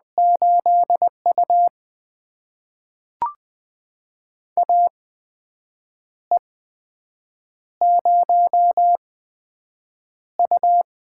Morse code, heard as a keyed tone, about 20 wpm, spaced as for 7 wpm, 700 Hz. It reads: 8U AE0U
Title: USA Call Signs - 4 Characters - 8x Character Spacing - 20wpm